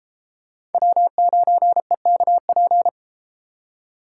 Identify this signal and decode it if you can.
Morse code, heard as a keyed tone, about 31 wpm, 700 Hz. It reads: W9EKP